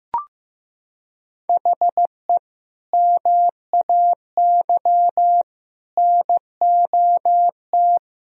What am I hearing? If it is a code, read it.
Morse code, heard as a keyed tone, about 16 wpm, 700 Hz. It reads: HE MAY NOT
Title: Sentences from Top 1000 Words - Review Part 2 - 15wpm